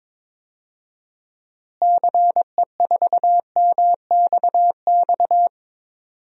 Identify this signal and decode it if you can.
Morse code, heard as a keyed tone, about 22 wpm, 700 Hz. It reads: CE4MXX
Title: DXCC-100 Call Signs - 6 Characters - Repeated 2x - 22wpm